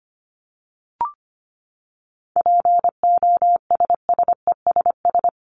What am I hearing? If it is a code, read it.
Morse code, heard as a keyed tone, about 25 wpm, 700 Hz. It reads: POSSESS
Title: Top 1751-2000 Words 25wpm